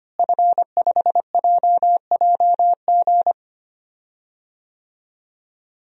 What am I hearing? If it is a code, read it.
Morse code, heard as a keyed tone, about 25 wpm, 700 Hz. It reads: F5JJG
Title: DXCC-100 Call Signs - 5 Characters - Repeated 2x - 25wpm